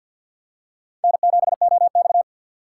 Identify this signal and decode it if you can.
Morse code, heard as a keyed tone, about 52 wpm, 700 Hz. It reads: N7OX
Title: USA Call Signs - 4x Character Spacing - 50wpm